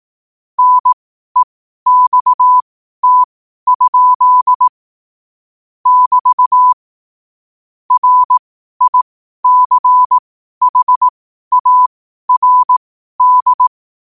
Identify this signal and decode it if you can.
Morse code, heard as a keyed tone, about 18 wpm, spaced as for 12 wpm, 1000 Hz. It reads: NEXT? <BT> RICHARD
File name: qotdfiles / 12wpm